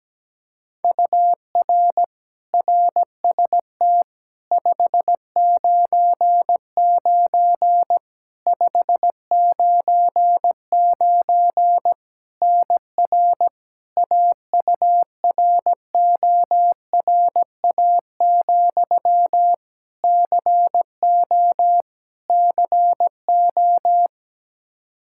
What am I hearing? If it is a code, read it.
Morse code, heard as a keyed tone, about 17 wpm, 700 Hz. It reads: UR RST 599 599 NR AURORA, CO CO
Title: K1USN SST QSO Practice 17wpm